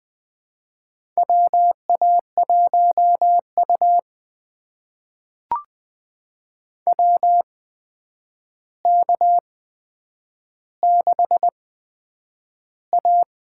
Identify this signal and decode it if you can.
Morse code, heard as a keyed tone, about 20 wpm, spaced as for 9 wpm, 700 Hz. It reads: WA1U WK6A
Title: USA Call Signs - 4 Characters - 8x Character Spacing - 20wpm